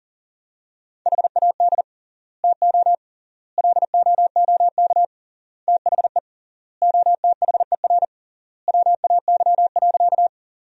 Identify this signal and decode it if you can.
Morse code, heard as a keyed tone, about 37 wpm, 700 Hz. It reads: HAD TO LOOK THE OTHER WAY.
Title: 3x Word-Spacing - Sentences from Top 100 Words - 40wpm